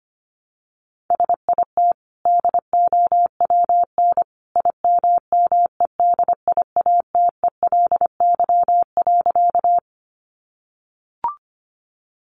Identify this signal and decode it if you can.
Morse code, heard as a keyed tone, about 25 wpm, 700 Hz. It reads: SIT DOWN IMMEDIATELY.
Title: Sentences from Top 1100 Words 25wpm